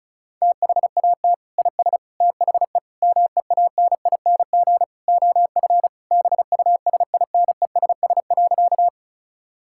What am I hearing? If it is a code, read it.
Morse code, heard as a keyed tone, about 34 wpm, 700 Hz. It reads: THAT IS THE MEANING OF BUSINESS.
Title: Sentences from Top 1000 Words - Review Part 6 - 35wpm